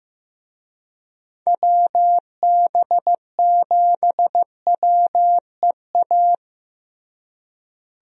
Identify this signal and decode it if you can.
Morse code, heard as a keyed tone, about 15 wpm, 700 Hz. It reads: WB7WEA
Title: USA Call Signs - 6 Characters - Repeated 2x - 15wpm